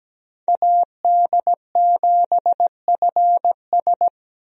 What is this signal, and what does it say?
Morse code, heard as a keyed tone, about 17 wpm, 700 Hz. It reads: AD7FS
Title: USA Call Signs - 5 Characters - Repeated 2x - 17wpm